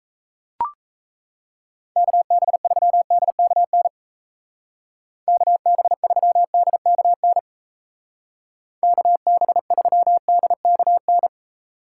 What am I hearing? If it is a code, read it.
Morse code, heard as a keyed tone, about 34 wpm, 700 Hz. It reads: KB3DKN KB3DKN KB3DKN
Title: Speed Racing - USA 2x3 Call Signs 28wpm